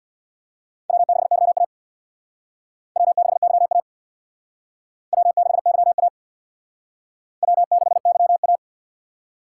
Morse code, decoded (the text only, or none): W6YA W6YA W6YA W6YA